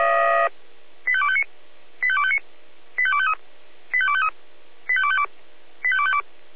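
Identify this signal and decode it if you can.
Five-tone selective call (ZVEI), heard as quick runs of five tones, tones of about 70 ms. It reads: 85259 85259 85253 85253 85252 85252
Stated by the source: ZVEI five-tone selective call received by SDR.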